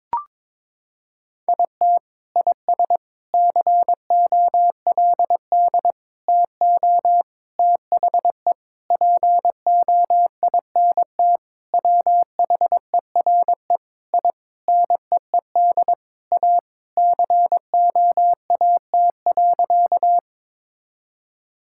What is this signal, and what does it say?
Morse code, heard as a keyed tone, about 22 wpm, 700 Hz. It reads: IT IS COLD TO THE POINT WHERE I NEED A COAT.
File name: practice / 22wpm / Sentences with 4-Word Phrases - 22wpm